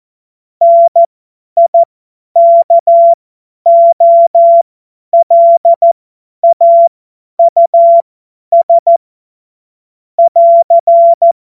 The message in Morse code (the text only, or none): NIKOLAUS <AR>